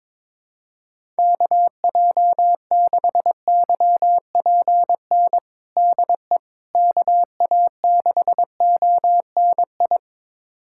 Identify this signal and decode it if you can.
Morse code, heard as a keyed tone, about 22 wpm, 700 Hz. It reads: KJ6YPN DE KA6ONI
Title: USA Callsign DE Callsign - 6 Characters - 22wpm